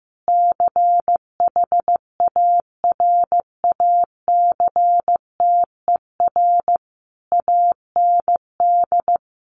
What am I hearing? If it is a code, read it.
Morse code, heard as a keyed tone, about 15 wpm, 700 Hz. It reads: CHARACTER AND
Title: Sentences from Top 1200 Words 15wpm